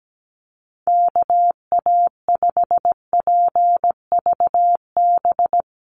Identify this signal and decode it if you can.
Morse code, heard as a keyed tone, about 17 wpm, 700 Hz. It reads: KA5PVB
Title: NA SOTA Chaser Callsigns 17wpm